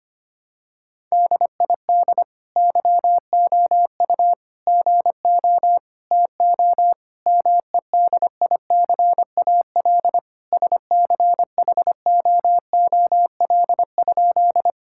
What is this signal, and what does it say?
Morse code, heard as a keyed tone, about 25 wpm, 700 Hz. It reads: DID YOU GO TO MEDICAL SCHOOL?
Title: Sentences from Top 1000 Words - Review Part 5 - 25wpm